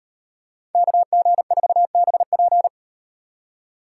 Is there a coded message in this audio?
Yes: Morse code from 0.7 to 2.7 s, heard as a keyed tone, about 38 wpm, 700 Hz. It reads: KG4BP